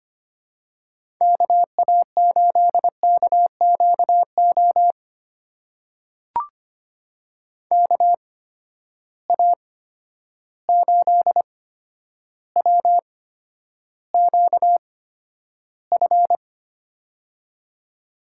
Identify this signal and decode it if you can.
Morse code, heard as a keyed tone, about 25 wpm, spaced as for 11 wpm, 700 Hz. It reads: KA8KQO KA8WQF
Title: USA Call Signs - 6 Characters - 8x Character Spacing - 25wpm